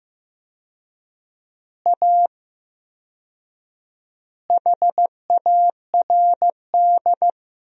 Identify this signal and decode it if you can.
Morse code, heard as a keyed tone, about 15 wpm, 700 Hz. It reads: A HARD
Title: 4x Word-Spacing - Sentences from Top 300 Words - 15wpm